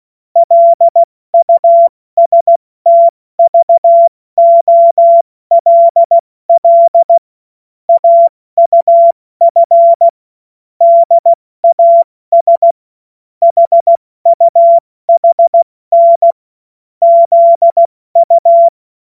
Morse code, heard as a keyed tone, about 16 wpm, spaced as for 14 wpm, 680 Hz. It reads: LUSTVOLL AUF DAS HUHN ZU
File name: Witz-15-Juli-2025-70-F80